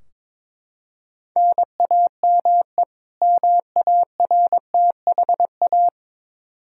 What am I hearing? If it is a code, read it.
Morse code, heard as a keyed tone, about 22 wpm, 700 Hz. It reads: NAME MARTHA
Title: Practice QSO - Exchange 1 - Beginning - 22wpm